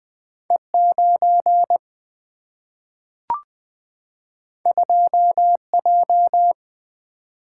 Morse code, heard as a keyed tone, about 20 wpm, 700 Hz. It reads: E 9 2 J
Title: Single-Letter Single-Number 20wpm